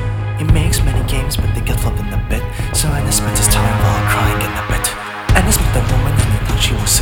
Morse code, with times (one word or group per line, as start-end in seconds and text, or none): none